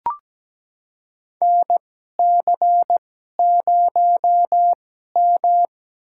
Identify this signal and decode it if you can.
Morse code, heard as a keyed tone, about 17 wpm, spaced as for 12 wpm, 700 Hz. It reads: NC0M
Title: USA Call Signs - 4 Characters - 2x Character Spacing - 17wpm